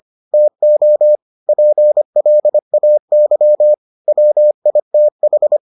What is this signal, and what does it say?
Morse code, heard as a keyed tone, about 25 wpm, 600 Hz. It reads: TO PLAY WITH